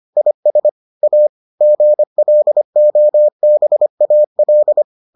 Morse code, heard as a keyed tone, about 25 wpm, 600 Hz. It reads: IS A GLOBAL